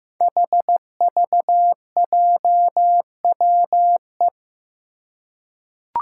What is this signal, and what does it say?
Morse code, heard as a keyed tone, about 15 wpm, 700 Hz. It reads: HVJWE
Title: Five Challenging Letter-Number Combinations 15wpm